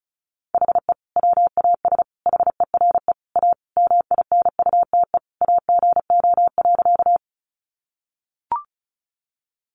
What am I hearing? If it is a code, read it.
Morse code, heard as a keyed tone, about 34 wpm, 700 Hz. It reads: HE WAS HERE A MINUTE AGO.
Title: Sentences from Top 500 Words 35wpm